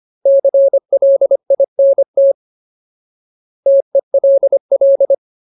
Morse code, heard as a keyed tone, about 25 wpm, 550 Hz. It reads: CLINT TELL
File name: Ultimate Word List / 25WPM Ultimate